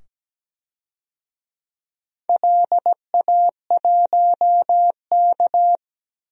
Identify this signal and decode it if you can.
Morse code, heard as a keyed tone, about 17 wpm, 700 Hz. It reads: LA1K
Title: DXCC-100 Call Signs - 4 Characters - 17wpm